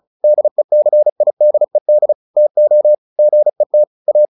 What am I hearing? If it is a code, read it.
Morse code, heard as a keyed tone, about 34 wpm, 600 Hz. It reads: DECIDED TO GET A